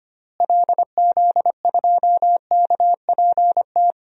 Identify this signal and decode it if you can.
Morse code, heard as a keyed tone, about 25 wpm, 700 Hz. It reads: LZ2KPT
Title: DXCC-100 Call Signs - 6 Characters - Repeated 3x - 25wpm